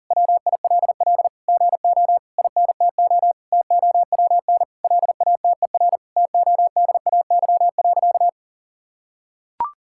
Morse code, heard as a keyed tone, about 40 wpm, 700 Hz. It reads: WILL GO INTO TOWN LATER TODAY.
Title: Sentences from Top 1000 Words - Review Part 6 - 40wpm